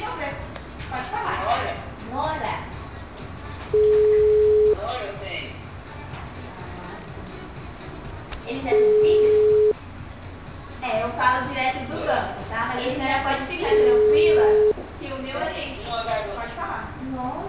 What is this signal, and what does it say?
Ringback tone (Europe, 425 Hz), with speech.
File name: out- -2001- - - .718